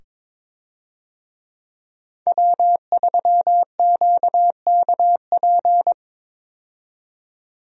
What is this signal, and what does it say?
Morse code, heard as a keyed tone, about 22 wpm, 700 Hz. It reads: W3QKP
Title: USA Call Signs - 5 Characters - 22wpm